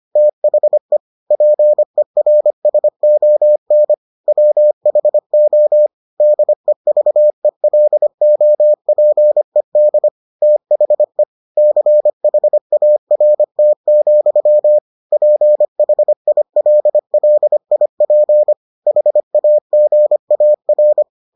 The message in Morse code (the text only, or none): THE PERSON WHO DEVELOPED THE CHART, PHILLIP HAGAR